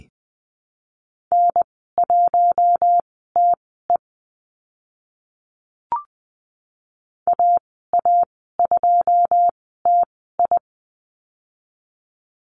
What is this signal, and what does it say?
Morse code, heard as a keyed tone, about 20 wpm, spaced as for 14 wpm, 700 Hz. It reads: N1TE AA2TI
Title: 082 Lesson 14g - Number 2 - Call Signs 20wpm